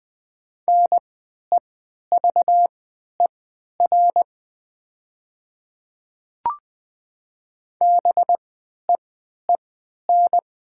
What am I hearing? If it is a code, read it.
Morse code, heard as a keyed tone, about 20 wpm, spaced as for 11 wpm, 690 Hz. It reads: NEVER BEEN
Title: 3x Character Spacing - Top 300 Words - 20wpm